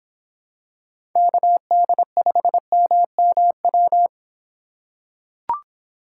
Morse code, heard as a keyed tone, about 24 wpm, 700 Hz. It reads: KD5MMW